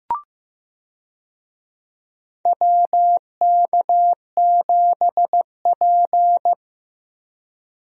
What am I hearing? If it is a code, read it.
Morse code, heard as a keyed tone, about 15 wpm, 700 Hz. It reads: WK7P
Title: USA Call Signs - 4 Characters - Repeated 2x - 15wpm